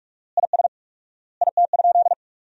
Morse code, heard as a keyed tone, about 46 wpm, 690 Hz. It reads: IS IT?
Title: 4x Word-Spacing - Sentences from Top 500 Words - 45wpm